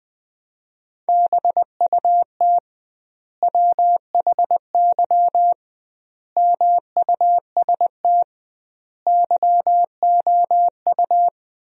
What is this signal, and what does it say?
Morse code, heard as a keyed tone, about 20 wpm, 700 Hz. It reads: BUT WHY MUST YOU
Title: 2x Word-Spacing - Sentences from Top 400 Words - 20wpm